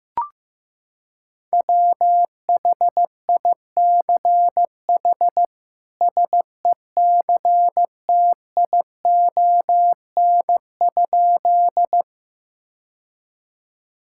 Morse code, heard as a keyed tone, about 16 wpm, 700 Hz. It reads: WHICH SECTION?
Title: Sentences from Top 1000 Words - Review Part 5 - 15wpm